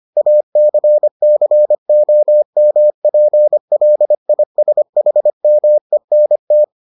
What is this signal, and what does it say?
Morse code, heard as a keyed tone, about 25 wpm, 600 Hz. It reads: ACCOMPLISHMENT